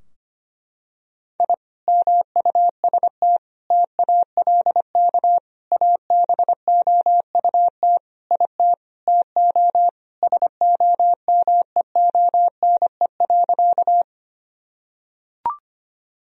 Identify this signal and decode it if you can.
Morse code, heard as a keyed tone, about 25 wpm, 700 Hz. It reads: I MUST TALK ABOUT IT TO SOMEONE.
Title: Sentences from Top 1000 Words - Review Part 4 - 25wpm